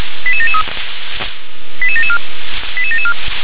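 Five-tone selective call (ZVEI), heard as quick runs of five tones, tones of about 70 ms. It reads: 88773 88774 88774
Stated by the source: Five-tone ZVEI selective call received by SDR.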